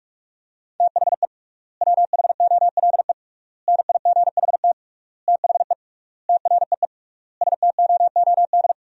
Morse code, heard as a keyed tone, about 48 wpm, 700 Hz. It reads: THE WHOLE NIGHT THE TREE STOOD
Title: 3x Word-Spacing - Sentences from Top 500 Words - 45wpm